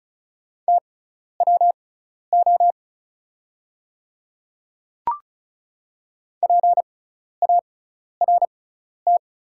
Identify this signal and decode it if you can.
Morse code, heard as a keyed tone, about 34 wpm, spaced as for 12 wpm, 700 Hz. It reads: TWO PART
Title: 6x Character Spacing - Top 200 Words - 35wpm